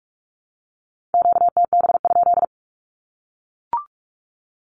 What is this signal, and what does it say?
Morse code, heard as a keyed tone, about 44 wpm, 700 Hz. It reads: QTB?
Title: Q-Codes Repeated 3x 45wpm